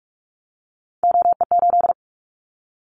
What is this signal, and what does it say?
Morse code, heard as a keyed tone, about 44 wpm, 700 Hz. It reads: OE8
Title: Callsign Prefixes for Top 25 DXCC - Repeated 3x - 45wpm